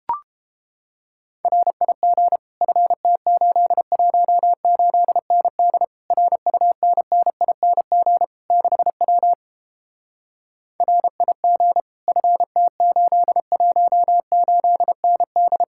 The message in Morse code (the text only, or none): RIG FT818ND RUNNING 6W RIG FT818ND